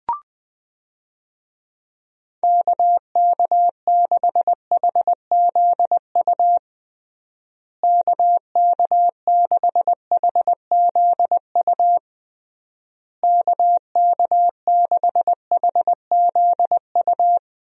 Morse code, heard as a keyed tone, about 20 wpm, 700 Hz. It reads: KK6HZU KK6HZU KK6HZU
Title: USA Call Signs - 6 Characters - Repeated 3x - 20wpm